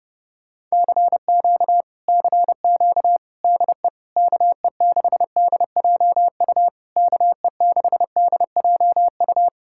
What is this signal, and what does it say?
Morse code, heard as a keyed tone, about 33 wpm, 700 Hz. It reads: CQ CQ DE KE6DJU KE6DJU